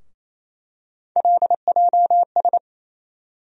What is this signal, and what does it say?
Morse code, heard as a keyed tone, about 27 wpm, 700 Hz. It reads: LJS
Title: USA Call Sign Postfixes - 3 Letters 28wpm